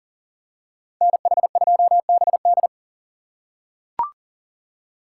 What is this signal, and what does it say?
Morse code, heard as a keyed tone, about 39 wpm, 700 Hz. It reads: NH2BD
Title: USA Call Signs - 5 Characters - 3x Character Spacing - 40wpm